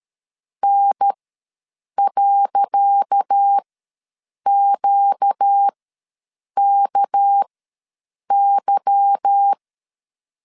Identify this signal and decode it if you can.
Morse code, heard as a keyed tone, about 13 wpm, spaced as for 7 wpm, 800 Hz. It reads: N.QKY
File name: LEZIONI / 10 SIGNS / 10C DOT ws 12wpm 3548